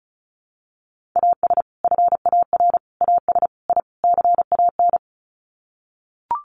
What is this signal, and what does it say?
Morse code, heard as a keyed tone, about 34 wpm, 700 Hz. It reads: AS FAR AS I CAN
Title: Sets of 5 Words Encore 35wpm